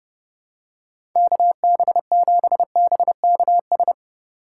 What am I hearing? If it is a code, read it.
Morse code, heard as a keyed tone, about 30 wpm, 700 Hz. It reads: KB7BKS
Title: USA Call Signs - 8x Character Spacing - 30wpm